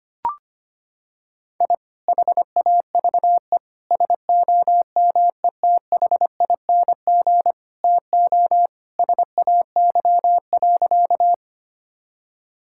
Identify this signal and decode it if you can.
Morse code, heard as a keyed tone, about 25 wpm, 700 Hz. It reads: I HAVE SOMETHING TO SAY.
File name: practice / 25wpm / Sentences with 2-Word Phrases - 25wpm